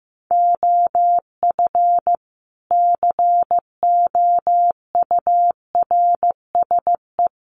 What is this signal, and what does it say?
Morse code, heard as a keyed tone, about 15 wpm, 700 Hz. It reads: OF COURSE
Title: A Sound Of Thunder 15wpm